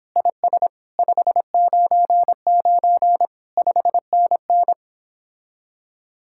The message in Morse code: IS 599 5NN

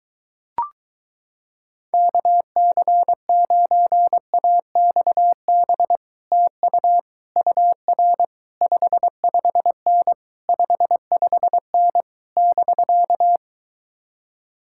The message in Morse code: KC9AXB TU UR 55N 55N <BK>

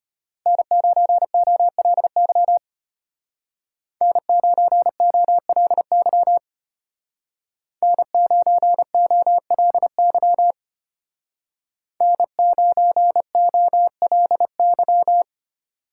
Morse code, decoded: N9OLY N9OLY N9OLY N9OLY

